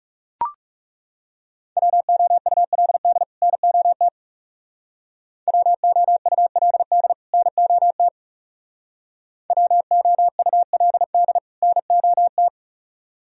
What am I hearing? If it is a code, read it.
Morse code, heard as a keyed tone, about 36 wpm, 700 Hz. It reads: WOULD NOT WOULD NOT WOULD NOT
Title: Speed-Racing - Sets of 2 Words 30wpm